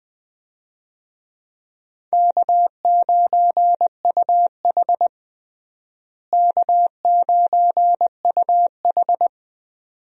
Morse code, heard as a keyed tone, about 20 wpm, 700 Hz. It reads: K9UH K9UH
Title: USA Call Signs - 4 Characters - Repeated 2x - 20wpm